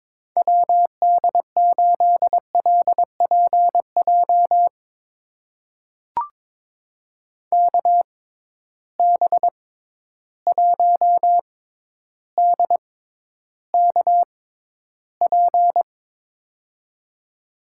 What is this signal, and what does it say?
Morse code, heard as a keyed tone, about 22 wpm, spaced as for 11 wpm, 700 Hz. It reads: WD8LPJ KB1DKP
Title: USA Call Signs - 6 Characters - 6x Character Spacing - 22wpm